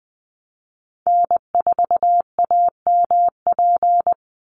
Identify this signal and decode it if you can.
Morse code, heard as a keyed tone, about 20 wpm, 700 Hz. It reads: N4AMP
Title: USA Call Signs 20wpm